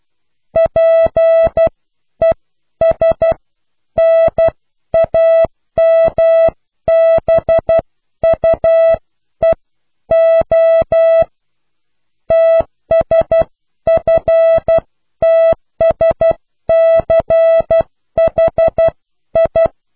Morse code, heard as a keyed tone, about 12 wpm, spaced as for 10 wpm, 660 Hz. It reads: PESNAMBUEO TSFTSCHI
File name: cw / pseudo worden(10)